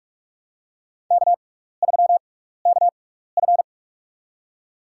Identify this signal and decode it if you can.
Morse code, heard as a keyed tone, about 46 wpm, spaced as for 15 wpm, 700 Hz. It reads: K3KF
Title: USA Call Signs - 4 Characters - 6x Character Spacing - 45wpm